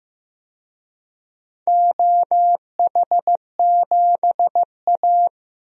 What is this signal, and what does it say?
Morse code, heard as a keyed tone, about 16 wpm, 700 Hz. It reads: OH7A
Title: DXCC-100 Call Signs - 4 Characters - Repeated 3x - 15wpm